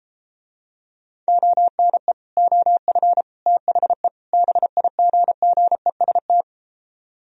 Morse code, heard as a keyed tone, about 33 wpm, 700 Hz. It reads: ONE OF THE BIGGEST